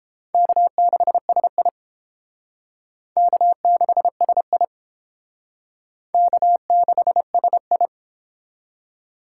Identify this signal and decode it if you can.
Morse code, heard as a keyed tone, about 31 wpm, 700 Hz. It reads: K6SI K6SI K6SI